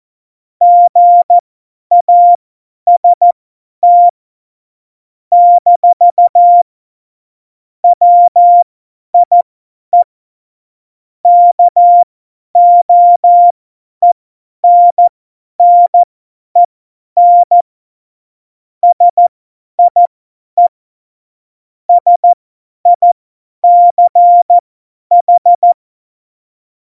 Morse code, heard as a keyed tone, about 14 wpm, spaced as for 10 wpm, 700 Hz. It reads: GAST - WIE KOENNEN SIE SICH
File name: Morsekurs / Witze / Witz 17.Dezember 2019, 50 F70